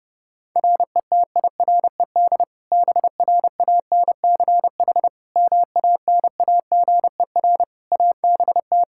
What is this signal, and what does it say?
Morse code, heard as a keyed tone, about 32 wpm, 700 Hz. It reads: RETIRED BRANCH MANAGER ABT
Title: K1USN SST QSO Practice 30wpm